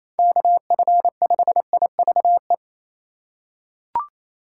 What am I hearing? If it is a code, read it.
Morse code, heard as a keyed tone, about 28 wpm, 700 Hz. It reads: KF5IVE
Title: USA Call Signs - 2x Character Spacing - 28wpm